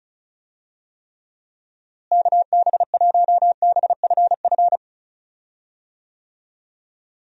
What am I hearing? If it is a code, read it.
Morse code, heard as a keyed tone, about 34 wpm, 700 Hz. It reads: KB1BFF
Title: USA Call Signs - 6 Characters - Repeated 3x - 35wpm